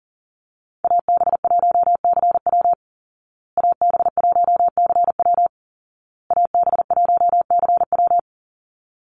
Morse code, heard as a keyed tone, about 37 wpm, 700 Hz. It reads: AB1CW AB1CW AB1CW
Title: Call Signs Repeated 2x - 40wpm